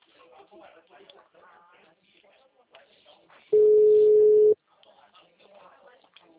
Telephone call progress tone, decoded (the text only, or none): ringback tone